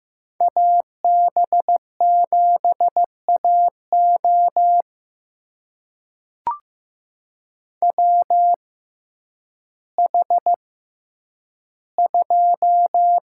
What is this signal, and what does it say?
Morse code, heard as a keyed tone, about 15 wpm, spaced as for 9 wpm, 700 Hz. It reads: AB7AO WH2
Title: USA Call Signs - 5 Characters - 6x Character Spacing - 15wpm